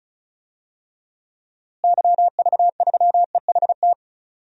Morse code, heard as a keyed tone, about 36 wpm, 700 Hz. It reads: YV3EHT
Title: DXCC-100 Call Signs - 6 Characters - Repeated 3x - 35wpm